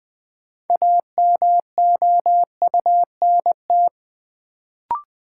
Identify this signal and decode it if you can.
Morse code, heard as a keyed tone, about 20 wpm, 700 Hz. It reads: AMOUNT
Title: Sets of 3 Words - New Word Drill - 20wpm